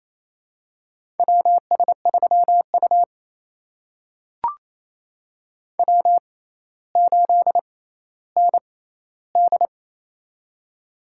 Morse code, heard as a keyed tone, about 28 wpm, spaced as for 14 wpm, 700 Hz. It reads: WS3U W8ND
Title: USA Call Signs - 4 Characters - 6x Character Spacing - 28wpm